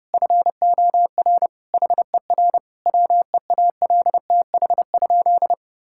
Morse code, heard as a keyed tone, about 32 wpm, 700 Hz. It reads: FOR HER WEALTH?
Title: Sentences from Top 1600 Words 30wpm